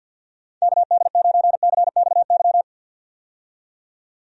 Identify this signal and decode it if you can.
Morse code, heard as a keyed tone, about 49 wpm, 700 Hz. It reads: KD9CXY
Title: USA Call Signs - 6 Characters - 50wpm